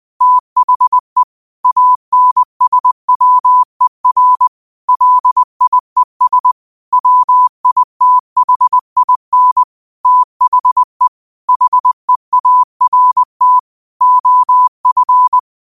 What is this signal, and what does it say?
Morse code, heard as a keyed tone, about 20 wpm, 1000 Hz. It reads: THE ANSWER LIES WITHIN THE HEART OF